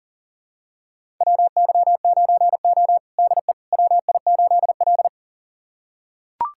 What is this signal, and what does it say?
Morse code, heard as a keyed tone, about 40 wpm, 700 Hz. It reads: WY9O DE WI8L
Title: USA Callsign DE Callsign - 4 Characters - 40wpm